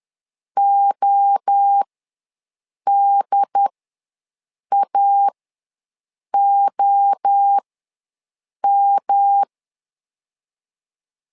Morse code, heard as a keyed tone, about 10 wpm, spaced as for 6 wpm, 800 Hz. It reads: ODAOM
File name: LEZIONI / 3 MWRO / 3 MWRO ws 10wpm 2310